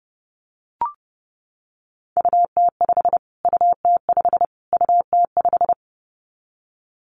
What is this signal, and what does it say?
Morse code, heard as a keyed tone, about 30 wpm, 700 Hz. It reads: UT5 UT5 UT5
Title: Callsign Prefixes for Top 25 DXCC - Repeated 3x - 30wpm